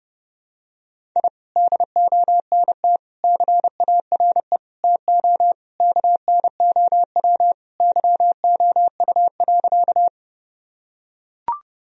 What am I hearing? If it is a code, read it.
Morse code, heard as a keyed tone, about 31 wpm, 700 Hz. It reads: I DONT CARE TO KNOW YOU.
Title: Sentences from Top 1000 Words - Review Part 1 - 30wpm